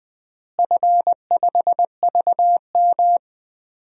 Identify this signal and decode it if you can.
Morse code, heard as a keyed tone, about 21 wpm, 700 Hz. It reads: F5VM
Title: DXCC-100 Call Signs - 4 Characters - Repeated 3x - 20wpm